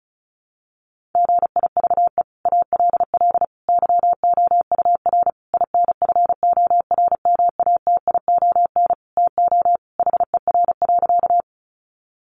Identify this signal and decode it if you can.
Morse code, heard as a keyed tone, about 34 wpm, 700 Hz. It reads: GIVE ALL YOUR INFORMATION TO HER.
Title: Sentences from Top 600 Words 35wpm